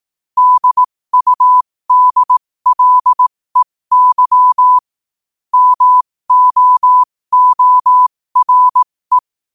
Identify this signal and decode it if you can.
Morse code, heard as a keyed tone, about 18 wpm, spaced as for 15 wpm, 1000 Hz. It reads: DUDLEY MOORE
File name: qotdfiles / 15wpm